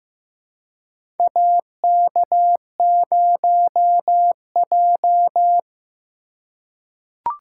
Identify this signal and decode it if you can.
Morse code, heard as a keyed tone, about 15 wpm, 700 Hz. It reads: AK0J
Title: USA Call Signs - 4 Characters - 2x Character Spacing - 15wpm